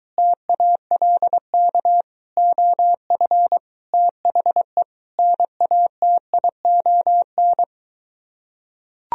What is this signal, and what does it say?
Morse code, heard as a keyed tone, about 23 wpm, 700 Hz. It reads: TALK OF THE NATION